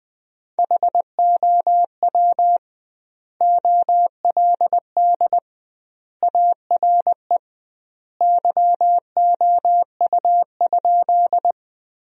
Morse code, heard as a keyed tone, about 21 wpm, 700 Hz. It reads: HOW OLD ARE YOU?
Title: 2x Word-Spacing - Sentences from Top 200 Words - 20wpm